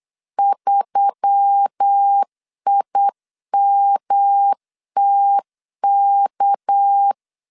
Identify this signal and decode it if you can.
Morse code, heard as a keyed tone, about 8 wpm, 800 Hz. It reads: 3IMTK
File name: LEZIONI / 6 PKX0 / 6 PK 8wpm 3003